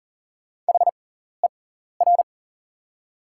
Morse code, heard as a keyed tone, about 38 wpm, spaced as for 14 wpm, 700 Hz. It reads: HER
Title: 6x Character Spacing - Top 200 Words - 40wpm